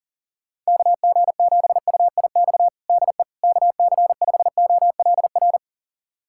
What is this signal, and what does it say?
Morse code, heard as a keyed tone, about 40 wpm, 700 Hz. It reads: KG7UIX DE KC5OLR
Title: USA Callsign DE Callsign - 6 Characters - 40wpm